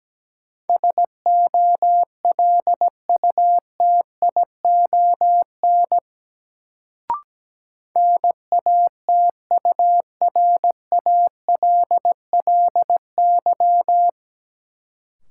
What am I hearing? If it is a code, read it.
Morse code, heard as a keyed tone, about 17 wpm, 700 Hz. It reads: SOLUTION NATURALLY
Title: Sentences from Top 1300 Words - New Word Drill Alt - 17wpm